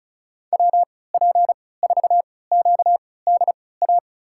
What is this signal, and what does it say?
Morse code, heard as a keyed tone, about 36 wpm, spaced as for 20 wpm, 700 Hz. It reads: WP4QDA
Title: USA Call Signs - 3x Character Spacing - 35wpm